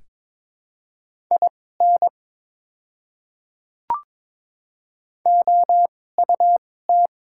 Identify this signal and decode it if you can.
Morse code, heard as a keyed tone, about 22 wpm, spaced as for 16 wpm, 700 Hz. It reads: IN OUT
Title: 2x Character Spacing - Top 100 Words - 22wpm